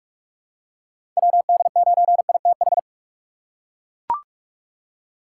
Morse code, heard as a keyed tone, about 41 wpm, 700 Hz. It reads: WD9ITH
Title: USA Call Signs - 6 Characters - 3x Character Spacing - 45wpm